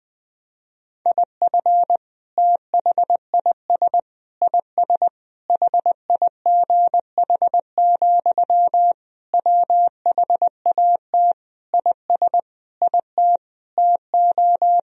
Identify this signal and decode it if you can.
Morse code, heard as a keyed tone, about 21 wpm, 700 Hz. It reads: IF THIS IS HIGH, WHAT IS IT TO
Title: Sentences from Top 1000 Words - Review Part 1 - 20wpm